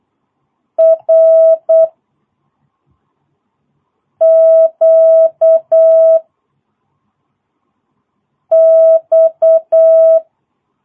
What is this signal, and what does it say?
Morse code, heard as a keyed tone, about 8 wpm, 660 Hz. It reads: R Q X
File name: jh8cbh / 2022cw